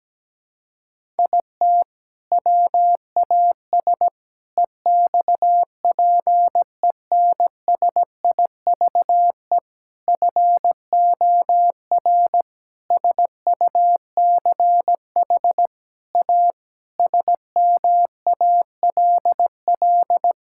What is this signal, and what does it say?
Morse code, heard as a keyed tone, about 17 wpm, 700 Hz. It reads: IT WAS EXPENSIVE FOR SUCH A SMALL
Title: Sentences from Top 1000 Words - Review Part 6 - 17wpm